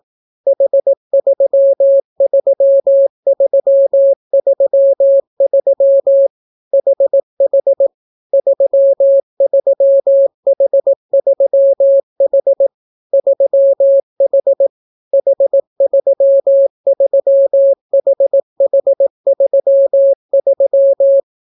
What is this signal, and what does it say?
Morse code, heard as a keyed tone, about 18 wpm, 550 Hz. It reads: H33333 HH 33H3H 3H H33HH33